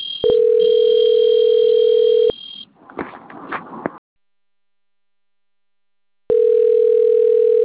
Ringback tone (North America), 440 + 480 Hz.
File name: rg-1600-unknown- - - .74690